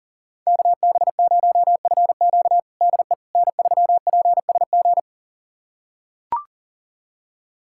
Morse code, heard as a keyed tone, about 40 wpm, 700 Hz. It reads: KB0FQ DE N3PSG
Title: USA Callsign DE Callsign - 5 Characters - 40wpm